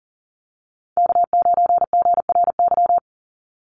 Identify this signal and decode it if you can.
Morse code, heard as a keyed tone, about 42 wpm, 700 Hz. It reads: K9GRY
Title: USA Call Signs 40wpm